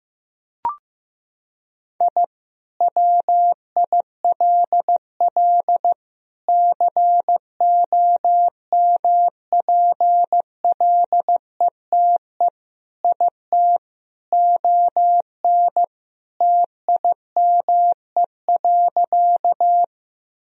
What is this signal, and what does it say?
Morse code, heard as a keyed tone, about 15 wpm, 700 Hz. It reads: I WILL COMPLETE IT ON TIME.
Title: Sentences from Top 1000 Words - Review Part 2 - 15wpm